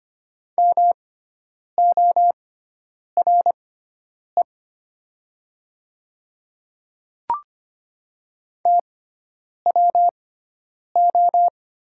Morse code, heard as a keyed tone, about 25 wpm, spaced as for 9 wpm, 700 Hz. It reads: MORE TWO
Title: 6x Character Spacing - Top 100 Words - 25wpm